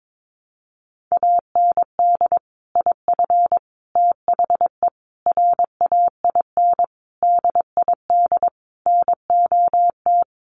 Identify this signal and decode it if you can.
Morse code, heard as a keyed tone, about 22 wpm, 700 Hz. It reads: AND IF THE RAIN DID NOT
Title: Sentences from Top 1300 Words 22wpm